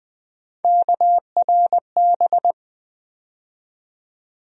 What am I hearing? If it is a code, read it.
Morse code, heard as a keyed tone, about 20 wpm, 700 Hz. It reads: KRB
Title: USA Call Sign Postfixes - 3 Letters 20wpm